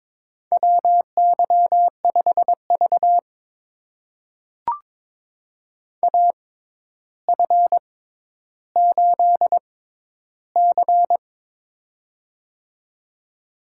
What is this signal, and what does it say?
Morse code, heard as a keyed tone, about 22 wpm, spaced as for 11 wpm, 700 Hz. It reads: WY5V AF8C
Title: USA Call Signs - 4 Characters - 6x Character Spacing - 22wpm